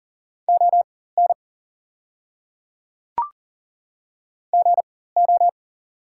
Morse code, heard as a keyed tone, about 40 wpm, spaced as for 19 wpm, 700 Hz. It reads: ON GO
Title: 4x Character Spacing - Top 100 Words - 40wpm